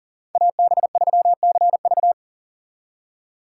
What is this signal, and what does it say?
Morse code, heard as a keyed tone, about 48 wpm, 700 Hz. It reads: AB3CV
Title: Speed Racing - USA 2x2 Call Signs 40wpm